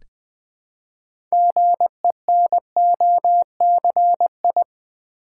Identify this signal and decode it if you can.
Morse code, heard as a keyed tone, about 20 wpm, 700 Hz. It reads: GENOCI
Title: Build A Word - 3 Letters At A Time - 8-Letter Words - 20wpm